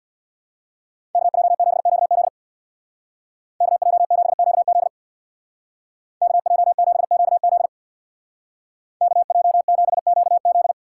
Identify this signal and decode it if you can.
Morse code, heard as a keyed tone, about 61 wpm, 690 Hz. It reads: KJ7QZ KJ7QZ KJ7QZ KJ7QZ
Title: Speed Racing - USA 2x2 Call Signs 50wpm